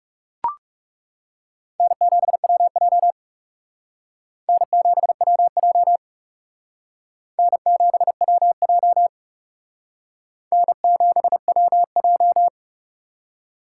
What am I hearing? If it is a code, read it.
Morse code, heard as a keyed tone, about 40 wpm, 700 Hz. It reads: N7WJ N7WJ N7WJ N7WJ